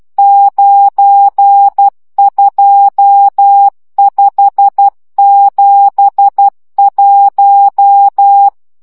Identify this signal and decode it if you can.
Morse code, heard as a keyed tone, about 12 wpm, 800 Hz. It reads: 92571
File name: morze vizsga 2011-2 10